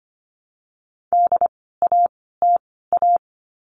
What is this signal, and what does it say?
Morse code, heard as a keyed tone, about 25 wpm, spaced as for 16 wpm, 700 Hz. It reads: DATA
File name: course / 25:10wpm / 094 Lesson 16f - Letter c - Words 25wpm